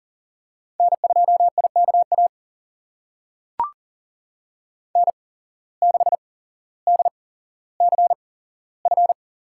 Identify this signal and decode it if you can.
Morse code, heard as a keyed tone, about 39 wpm, spaced as for 17 wpm, 700 Hz. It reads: N2IKA N6DCF
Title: USA Call Signs - 5 Characters - 8x Character Spacing - 40wpm